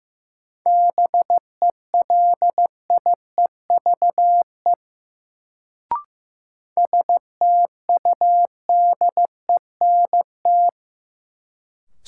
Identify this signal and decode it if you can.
Morse code, heard as a keyed tone, about 15 wpm, 700 Hz. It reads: BELIEVE STUDENT
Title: Sentences from Top 1000 Words - New Word Drill - 15wpm